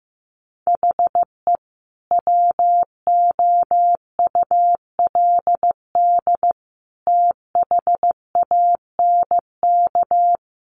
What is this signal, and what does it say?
Morse code, heard as a keyed tone, about 15 wpm, 700 Hz. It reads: HE WOULD THANK
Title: Sentences from Top 1100 Words 15wpm